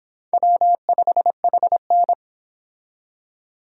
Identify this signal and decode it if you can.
Morse code, heard as a keyed tone, about 29 wpm, 700 Hz. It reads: W5HN